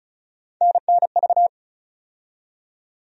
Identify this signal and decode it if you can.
Morse code, heard as a keyed tone, about 36 wpm, 700 Hz. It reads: NNV
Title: USA Call Sign Postfixes - 3 Letters 35wpm